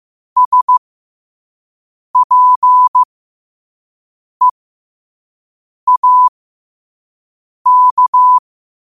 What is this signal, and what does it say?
Morse code, heard as a keyed tone, about 15 wpm, spaced as for 5 wpm, 1000 Hz. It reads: SPEAK
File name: qotdfiles / 5wpm